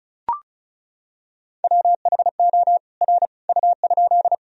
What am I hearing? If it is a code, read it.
Morse code, heard as a keyed tone, about 36 wpm, 700 Hz. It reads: WHO R U?
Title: Sentences from Top 100 Words and Abbreviations 35wpm